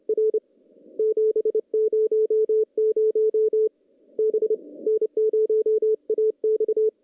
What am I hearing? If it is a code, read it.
Morse code, heard as a keyed tone, about 29 wpm, 440 Hz. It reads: R 700 B N0AX
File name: contests / CW-SS-k5tr- -0600